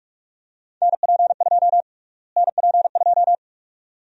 Morse code, heard as a keyed tone, about 42 wpm, 700 Hz. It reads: NP2 NP2
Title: USA Call Sign Prefixes - Repeated 2x - 45wpm